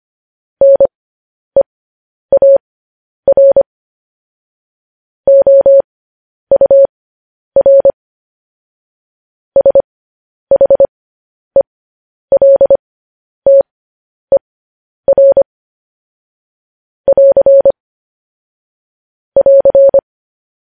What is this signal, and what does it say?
Morse code, heard as a keyed tone, about 25 wpm, spaced as for 10 wpm, 550 Hz. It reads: NEAR OUR SHELTER <AR> <AR>